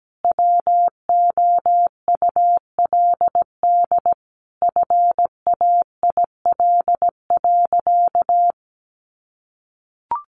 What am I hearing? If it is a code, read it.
Morse code, heard as a keyed tone, about 17 wpm, 700 Hz. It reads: WOULD FAIL.
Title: Sentences from Top 1100 Words 17wpm